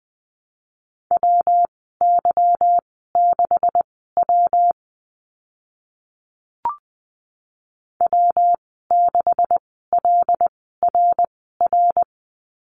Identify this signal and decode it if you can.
Morse code, heard as a keyed tone, about 20 wpm, spaced as for 15 wpm, 700 Hz. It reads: WY6W W6LRR
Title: 167 Lesson 27g - Number 7 - Call Signs 20wpm